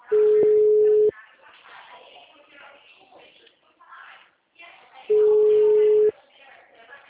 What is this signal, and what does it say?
Ringback tone (Europe, 425 Hz), with speech.